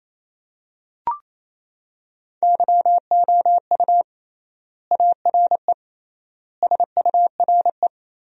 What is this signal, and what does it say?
Morse code, heard as a keyed tone, about 28 wpm, 700 Hz. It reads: YOU ARE SURE
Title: 3x Word-Spacing - Sentences from Top 300 Words - 28wpm